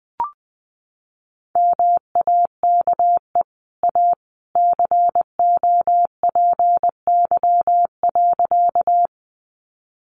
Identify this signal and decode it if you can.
Morse code, heard as a keyed tone, about 20 wpm, 700 Hz. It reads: MAKE A COPY.
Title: Sentences from Top 1400 Words 20wpm